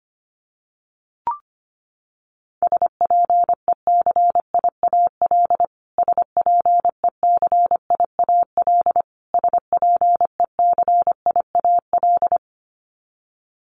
Morse code, heard as a keyed tone, about 25 wpm, 700 Hz. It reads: SPECIAL SPECIAL SPECIAL
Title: Top 500 Words Repeated 25wpm